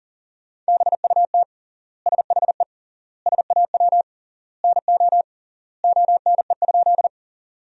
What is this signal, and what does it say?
Morse code, heard as a keyed tone, about 40 wpm, 700 Hz. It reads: BUT SHE SAW NO ONE?
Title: 3x Word-Spacing - Sentences from Top 300 Words - 40wpm